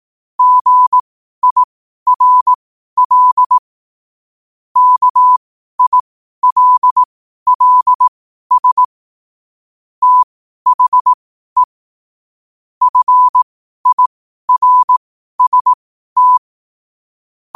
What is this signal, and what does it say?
Morse code, heard as a keyed tone, about 18 wpm, spaced as for 12 wpm, 1000 Hz. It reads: GIRL KILLS THE FIRST
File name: qotdfiles / 12wpm